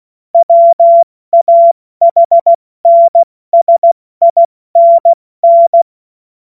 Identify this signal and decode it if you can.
Morse code, heard as a keyed tone, about 16 wpm, 680 Hz. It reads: WAHNSINN
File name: Morsekurs / Witze / Witz 02 April 2024 70 F80